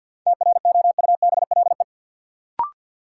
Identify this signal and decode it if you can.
Morse code, heard as a keyed tone, about 48 wpm, 700 Hz. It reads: TROUBLE
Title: Sentences from Top 1200 Words - New Word Drill - 50wpm